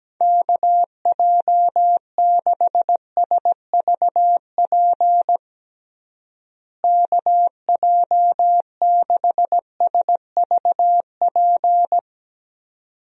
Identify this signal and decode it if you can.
Morse code, heard as a keyed tone, about 17 wpm, 700 Hz. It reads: KJ6SVP KJ6SVP
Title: USA Call Signs - 6 Characters - Repeated 2x - 17wpm